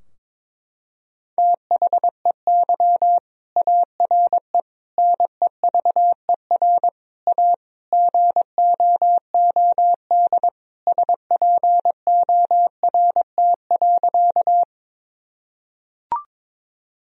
Morse code, heard as a keyed tone, about 22 wpm, 700 Hz. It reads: THEY ARE NEVER A GOOD SPORT.
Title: Sentences from Top 1000 Words - Review Part 4 - 22wpm